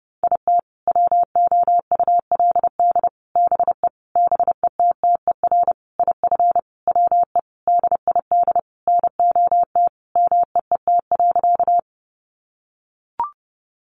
Morse code, heard as a keyed tone, about 30 wpm, 700 Hz. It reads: IT WOULD BE BETTER IF WE DID NOT MEET.
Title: Sentences from Top 1100 Words 30wpm